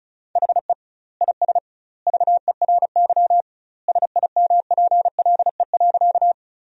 Morse code, heard as a keyed tone, about 34 wpm, 700 Hz. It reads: HE IS VERY SIMPLE.
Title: 2x Word-Spacing - Sentences from Top 400 Words - 35wpm